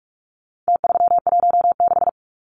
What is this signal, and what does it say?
Morse code, heard as a keyed tone, about 44 wpm, 700 Hz. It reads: T316
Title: Numbers - Contest Style - 4-Digit 45wpm